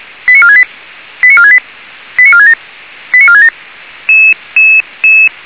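Five-tone selective call (ZVEI), heard as quick runs of five tones, tones of about 70 ms. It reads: 89468 89468 89467 89467
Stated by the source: ZVEI five-tone call 89355 received on SDR.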